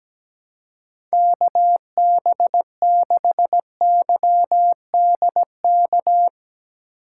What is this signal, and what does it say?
Morse code, heard as a keyed tone, about 17 wpm, 700 Hz. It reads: KB6YDK
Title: USA Call Signs - 6 Characters - 3x Character Spacing - 17wpm